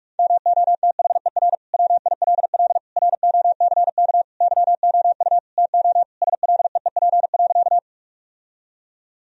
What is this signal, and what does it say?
Morse code, heard as a keyed tone, about 46 wpm, 700 Hz. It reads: MOTHER WILL ROCK YOU TO SLEEP.
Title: Sentences from Top 1000 Words - Review Part 2 - 45wpm